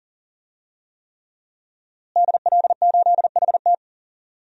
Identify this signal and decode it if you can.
Morse code, heard as a keyed tone, about 40 wpm, 700 Hz. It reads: DL8HT
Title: DXCC-100 Call Signs - 5 Characters - Repeated 3x - 40wpm